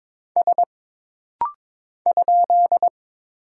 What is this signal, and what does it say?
Morse code, heard as a keyed tone, about 22 wpm, 700 Hz. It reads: S ?
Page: This rapid-fire timing is geared to push your ICR (Instant Character Recognition) skills.